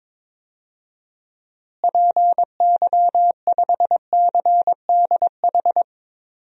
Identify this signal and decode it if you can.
Morse code, heard as a keyed tone, about 22 wpm, 700 Hz. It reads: PY5CDH
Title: DXCC-100 Call Signs - 6 Characters - Repeated 3x - 22wpm